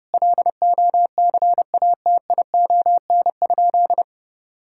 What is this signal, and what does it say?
Morse code, heard as a keyed tone, about 30 wpm, 700 Hz. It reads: LOCATION?
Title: Sentences from Top 1000 Words - Review Part 6 - 30wpm